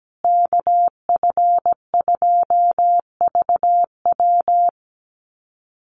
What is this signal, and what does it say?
Morse code, heard as a keyed tone, about 17 wpm, 700 Hz. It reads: KF2VW
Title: USA Call Signs Repeated 3x 17wpm